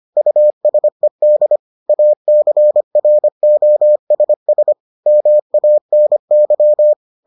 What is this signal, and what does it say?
Morse code, heard as a keyed tone, about 25 wpm, 600 Hz. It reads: USED ACROSS MANY